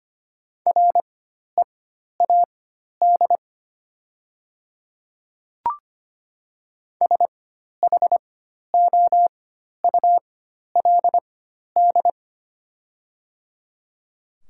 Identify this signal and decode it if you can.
Morse code, heard as a keyed tone, about 25 wpm, spaced as for 12 wpm, 700 Hz. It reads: READ SHOULD
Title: 4x Character Spacing - Top 500 Words - 25wpm